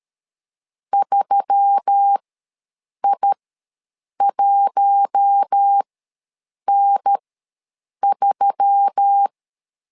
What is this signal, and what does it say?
Morse code, heard as a keyed tone, about 13 wpm, spaced as for 7 wpm, 800 Hz. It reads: 3I1N3
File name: LEZIONI / 4 Q1S3 ws 12wpm 2038